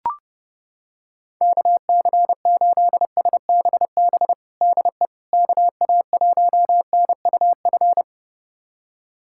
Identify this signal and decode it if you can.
Morse code, heard as a keyed tone, about 29 wpm, 700 Hz. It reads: KC8SBB DE KA1NUF
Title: USA Callsign DE Callsign - 6 Characters - 30wpm